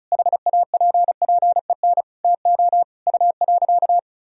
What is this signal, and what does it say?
Morse code, heard as a keyed tone, about 36 wpm, 700 Hz. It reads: HAPPEN TO U.
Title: Sentences from Top 500 Words and Abbreviations 35wpm